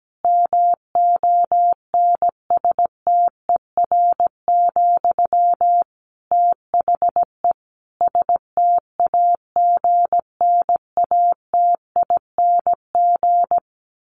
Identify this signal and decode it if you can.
Morse code, heard as a keyed tone, about 17 wpm, 700 Hz. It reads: MONSTER, THE STAGNATING
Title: A Sound Of Thunder 17wpm